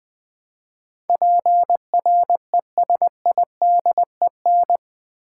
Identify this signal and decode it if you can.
Morse code, heard as a keyed tone, about 20 wpm, 700 Hz. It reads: PRESIDEN
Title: Build A Word - 4 Letters At A Time - 10-Letter Words - 20wpm